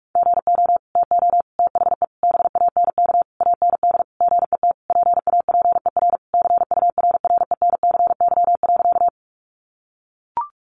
Morse code, heard as a keyed tone, about 44 wpm, 700 Hz. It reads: GO TO THE BANK AND GET PAPER CURRENCY.
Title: Sentences from Top 1500 Words 45wpm